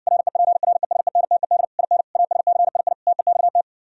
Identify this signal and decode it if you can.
Morse code, heard as a keyed tone, about 59 wpm, 700 Hz. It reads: REPRESENTED IN ASCII TEXT